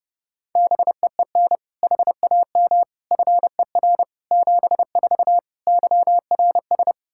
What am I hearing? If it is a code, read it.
Morse code, heard as a keyed tone, about 30 wpm, 700 Hz. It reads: BEEN HAM FER 74 YRS
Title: K1USN SST QSO Practice 30wpm